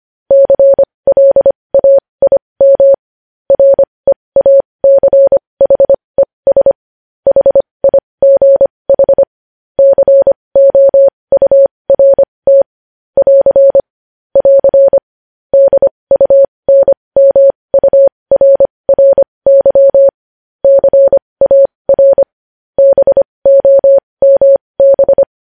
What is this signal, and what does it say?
Morse code, heard as a keyed tone, about 25 wpm, spaced as for 20 wpm, 550 Hz. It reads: CLAIM REACHES HIGH COURT <AR> <AR> DUNMURRY CAR BOMB
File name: podcast / News headlines-1x-25x20